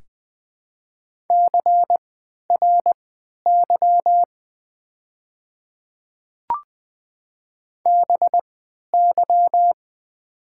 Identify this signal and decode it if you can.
Morse code, heard as a keyed tone, about 20 wpm, spaced as for 11 wpm, 700 Hz. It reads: CRY BY
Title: 3x Character Spacing - Top 500 Words - 20wpm